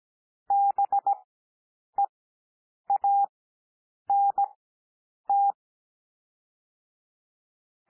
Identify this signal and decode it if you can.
Morse code, heard as a keyed tone, about 17 wpm, spaced as for 8 wpm, 800 Hz. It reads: BEANT
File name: cwmp3 / L2 SOUND LETTERS 17 wpm